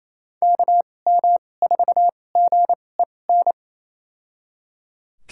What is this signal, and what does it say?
Morse code, heard as a keyed tone, about 28 wpm, spaced as for 20 wpm, 700 Hz. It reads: KM4GEN
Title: USA Call Signs - 6 Characters - 2x Character Spacing - 28wpm